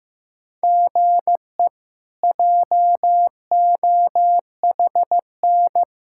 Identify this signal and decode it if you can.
Morse code, heard as a keyed tone, about 15 wpm, 700 Hz. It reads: GE JOHN
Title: K1USN SST QSO Practice - 15wpm